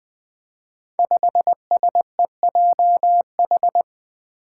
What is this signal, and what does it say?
Morse code, heard as a keyed tone, about 20 wpm, 700 Hz. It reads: 5SEJH
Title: Five Challenging Letter-Number Combinations 20wpm